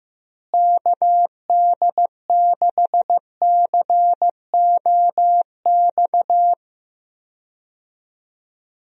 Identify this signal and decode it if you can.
Morse code, heard as a keyed tone, about 15 wpm, 700 Hz. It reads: KD6COX
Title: USA Call Signs - 6 Characters - Repeated 2x - 15wpm